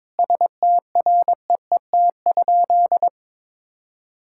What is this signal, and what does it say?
Morse code, heard as a keyed tone, about 22 wpm, 700 Hz. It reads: STREET?
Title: Sentences from Top 1600 Words 22wpm